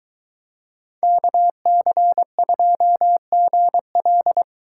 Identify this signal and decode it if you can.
Morse code, heard as a keyed tone, about 22 wpm, 700 Hz. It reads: KC2GL